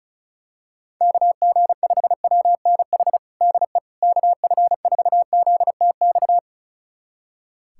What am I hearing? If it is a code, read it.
Morse code, heard as a keyed tone, about 36 wpm, 700 Hz. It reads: KG5WNH DE KF4ZTX
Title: USA Callsign DE Callsign 35wpm